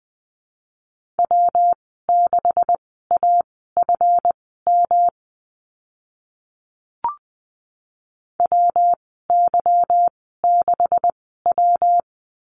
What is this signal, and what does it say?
Morse code, heard as a keyed tone, about 20 wpm, spaced as for 15 wpm, 700 Hz. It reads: W6AFM WY6W
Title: 147 Lesson 24g - Letter y - Call Signs 20wpm